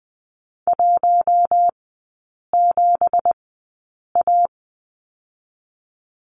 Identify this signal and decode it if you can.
Morse code, heard as a keyed tone, about 20 wpm, spaced as for 8 wpm, 700 Hz. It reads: 17A
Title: 166 Lesson 27e - Number 7 - Three Characters 20wpm